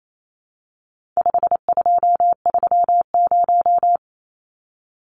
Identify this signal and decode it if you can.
Morse code, heard as a keyed tone, about 28 wpm, 700 Hz. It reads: 5230
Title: Numbers - 4-Digit 28wpm